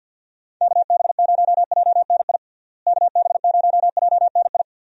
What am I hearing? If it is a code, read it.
Morse code, heard as a keyed tone, about 52 wpm, 700 Hz. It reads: KB0JNI KB0JNI
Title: USA Call Signs - 6 Characters - Repeated 2x - 50wpm